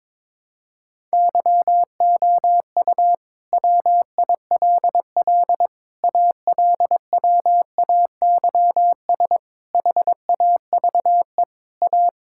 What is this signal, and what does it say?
Morse code, heard as a keyed tone, about 22 wpm, 700 Hz. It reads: YOU WILL ALWAYS HAVE A
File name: Sentences with 9-Word Phrases - 22wpm